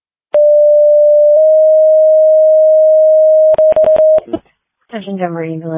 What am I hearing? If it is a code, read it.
Two-tone page, heard as a one-second tone, then a three-second tone, about 590 then 620 Hz, with speech. It is heard from 0.3 to 4.2 s.